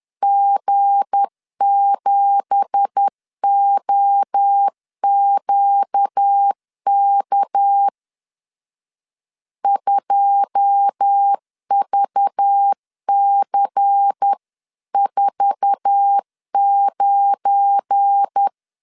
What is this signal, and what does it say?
Morse code, heard as a keyed tone, about 10 wpm, 800 Hz. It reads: G7OQK 2VC49